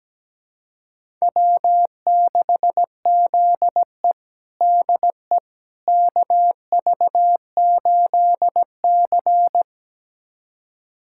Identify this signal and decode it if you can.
Morse code, heard as a keyed tone, about 17 wpm, 700 Hz. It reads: W6ZE DE KV8C
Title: USA Callsign DE Callsign - 4 Characters - 17wpm